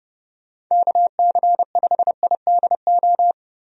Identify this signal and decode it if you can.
Morse code, heard as a keyed tone, about 30 wpm, 700 Hz. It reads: KC5IDO
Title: USA Call Signs - 6 Characters - 30wpm